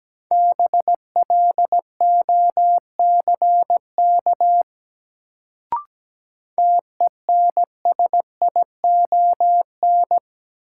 Morse code, heard as a keyed tone, about 17 wpm, 700 Hz. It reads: BLOCK TENSION
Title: Sentences from Top 1600 Words - New Word Drill - 17wpm